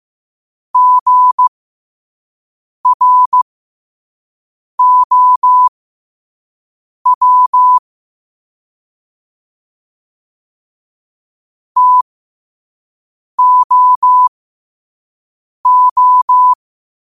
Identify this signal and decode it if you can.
Morse code, heard as a keyed tone, about 15 wpm, spaced as for 5 wpm, 1000 Hz. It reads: GROW TOO
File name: qotdfiles / 5wpm